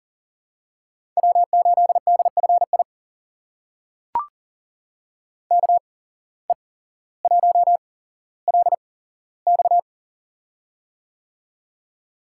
Morse code, heard as a keyed tone, about 40 wpm, spaced as for 17 wpm, 700 Hz. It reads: W8DFI KE1LX
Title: USA Call Signs - 5 Characters - 8x Character Spacing - 40wpm